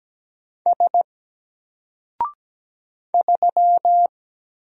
Morse code, heard as a keyed tone, about 17 wpm, 700 Hz. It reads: S 3
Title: Single Letter-Number Alt 17wpm